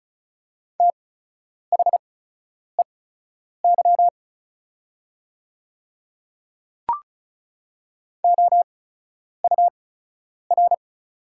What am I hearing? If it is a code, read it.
Morse code, heard as a keyed tone, about 36 wpm, spaced as for 10 wpm, 700 Hz. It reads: THEY OUR
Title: 8x Character Spacing - Top 200 Words - 35wpm